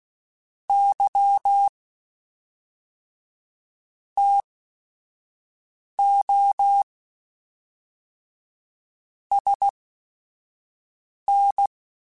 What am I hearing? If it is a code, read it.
Morse code, heard as a keyed tone, about 16 wpm, spaced as for 4 wpm, 800 Hz. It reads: YTOSN
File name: QSOs / QSOs-mp3s / WA7QOX 1@5